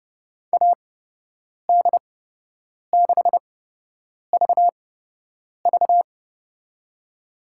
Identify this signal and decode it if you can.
Morse code, heard as a keyed tone, about 29 wpm, spaced as for 8 wpm, 700 Hz. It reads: AD6VV
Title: USA Call Signs - 5 Characters - 8x Character Spacing - 30wpm